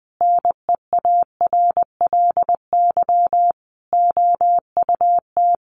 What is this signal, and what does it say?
Morse code, heard as a keyed tone, about 20 wpm, 700 Hz. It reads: NEARLY OUT
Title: Sentences from Top 1300 Words 20wpm